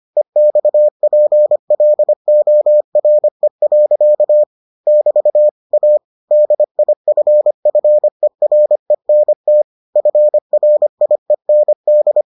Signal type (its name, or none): Morse code